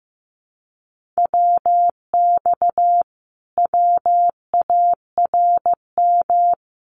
Morse code, heard as a keyed tone, about 15 wpm, 700 Hz. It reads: WX WARM
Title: Common QSO Phrases Encore 15wpm